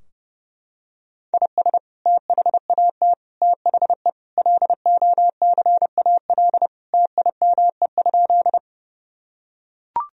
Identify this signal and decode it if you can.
Morse code, heard as a keyed tone, about 30 wpm, 700 Hz. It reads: IS THAT THE LOCAL TIME?
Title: Sentences from Top 1000 Words - Review Part 3 - 30wpm